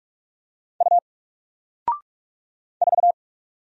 Morse code, heard as a keyed tone, about 44 wpm, 700 Hz. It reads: U 4
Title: Single Character (Letter:Number:Punctuation:Prosign) Alt 45wpm